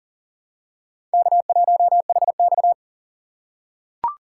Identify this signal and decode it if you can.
Morse code, heard as a keyed tone, about 38 wpm, 700 Hz. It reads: K1HX
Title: USA Call Signs - 4 Characters - 6x Character Spacing - 40wpm